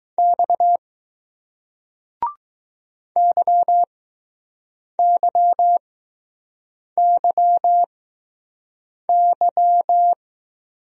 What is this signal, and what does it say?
Morse code, heard as a keyed tone, about 20 wpm, spaced as for 6 wpm, 700 Hz. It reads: X YYYY